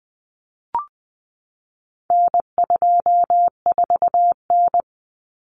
Morse code, heard as a keyed tone, about 20 wpm, 700 Hz. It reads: N24N
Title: Common Cut Numbers - 4-Digit 20wpm